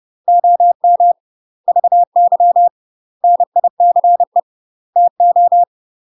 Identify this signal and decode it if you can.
Morse code, heard as a keyed tone, about 30 wpm, 700 Hz. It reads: OM VY NICE TO
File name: UK MASTER QSOs TRAINING / 30WPM / UK MASTER QSOs 0035